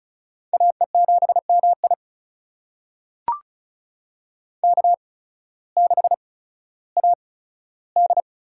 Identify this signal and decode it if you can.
Morse code, heard as a keyed tone, about 36 wpm, spaced as for 16 wpm, 700 Hz. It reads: AE7MI K6AD
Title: USA Call Signs - 5 Characters - 8x Character Spacing - 35wpm